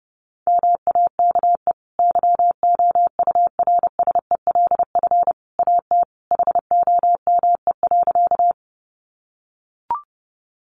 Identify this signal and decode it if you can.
Morse code, heard as a keyed tone, about 30 wpm, 700 Hz. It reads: MAKE YOURSELF AT HOME.
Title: Sentences from Top 600 Words 30wpm